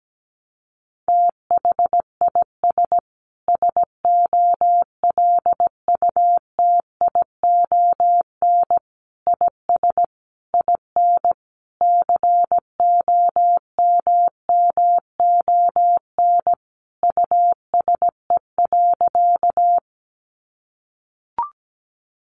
Morse code, heard as a keyed tone, about 17 wpm, 700 Hz. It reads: THIS SOLUTION IS IN COMMON USE.
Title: Sentences from Top 1300 Words 17wpm